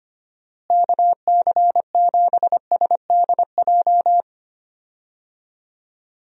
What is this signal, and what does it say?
Morse code, heard as a keyed tone, about 25 wpm, 700 Hz. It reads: KC7SDJ
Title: USA Call Signs - 6 Characters - Repeated 3x - 25wpm